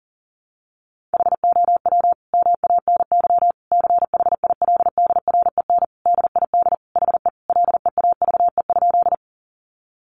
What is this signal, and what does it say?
Morse code, heard as a keyed tone, about 40 wpm, 700 Hz. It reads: HOW MANY CHILDREN DID HE LEAVE?
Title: Sentences from Top 400 Words 40wpm